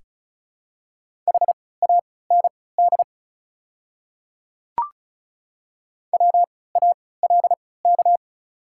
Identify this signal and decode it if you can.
Morse code, heard as a keyed tone, about 34 wpm, spaced as for 20 wpm, 700 Hz. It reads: HAND WALK
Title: 3x Character Spacing - Top 300 Words - 35wpm